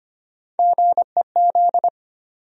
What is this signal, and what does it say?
Morse code, heard as a keyed tone, about 25 wpm, 700 Hz. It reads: GEZ
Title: USA Call Sign Postfixes - 3 Letters 25wpm